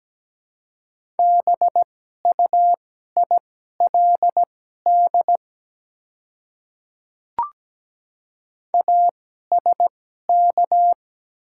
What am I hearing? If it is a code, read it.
Morse code, heard as a keyed tone, about 17 wpm, spaced as for 12 wpm, 700 Hz. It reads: BUILD ASK
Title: 2x Character Spacing - Top 200 Words - 17wpm